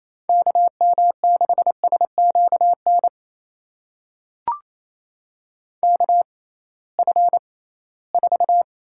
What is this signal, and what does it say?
Morse code, heard as a keyed tone, about 28 wpm, spaced as for 18 wpm, 700 Hz. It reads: KM6SQN KF4
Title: USA Call Signs - 6x Character Spacing - 28wpm